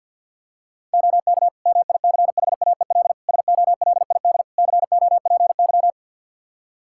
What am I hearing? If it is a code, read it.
Morse code, heard as a keyed tone, about 47 wpm, 700 Hz. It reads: OK MICHAEL SOLID COPY
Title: K1USN SST QSO Practice 50wpm